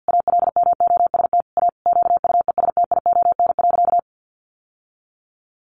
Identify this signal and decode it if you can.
Morse code, heard as a keyed tone, about 50 wpm, 700 Hz. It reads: ALMOST A QUESTION.
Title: Sentences from Top 1200 Words 50wpm